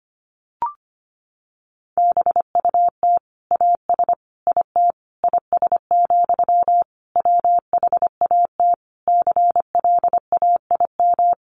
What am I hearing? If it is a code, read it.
Morse code, heard as a keyed tone, about 25 wpm, 700 Hz. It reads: BUT AS IT IS, WHAT CLAIM
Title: Sentences from Top 1500 Words 25wpm